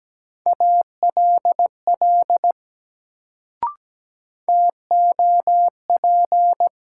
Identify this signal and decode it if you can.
Morse code, heard as a keyed tone, about 17 wpm, 700 Hz. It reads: ALL TOP
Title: Sets of 2 Words - New Word Drill - 17wpm